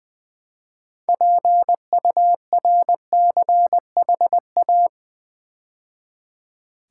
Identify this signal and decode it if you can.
Morse code, heard as a keyed tone, about 20 wpm, 700 Hz. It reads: PURCHA
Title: Build A Word - 2 Letters At A Time - 8-Letter Words - 20wpm